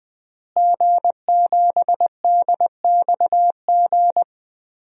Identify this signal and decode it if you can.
Morse code, heard as a keyed tone, about 20 wpm, 700 Hz. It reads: G7DXG
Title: DXCC-100 Call Signs - 5 Characters - Repeated 3x - 20wpm